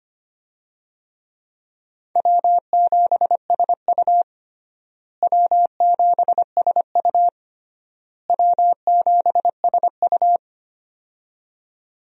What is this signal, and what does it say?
Morse code, heard as a keyed tone, about 25 wpm, 700 Hz. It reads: W7SU W7SU W7SU
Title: USA Call Signs - 4 Characters - Repeated 3x - 25wpm